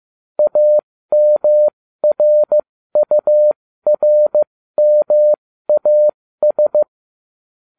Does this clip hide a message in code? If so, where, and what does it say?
Yes: Morse code from 0.4 to 6.8 s, heard as a keyed tone, about 15 wpm, spaced as for 13 wpm, 600 Hz. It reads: AMRURMAS